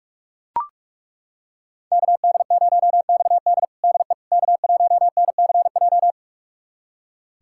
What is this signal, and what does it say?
Morse code, heard as a keyed tone, about 42 wpm, 700 Hz. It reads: KD0XD DE K1NCJ
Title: USA Callsign DE Callsign - 5 Characters - 45wpm